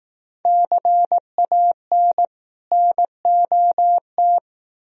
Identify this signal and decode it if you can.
Morse code, heard as a keyed tone, about 18 wpm, 700 Hz. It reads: CAN NOT
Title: Speed-Racing - Sets of 2 Words 15wpm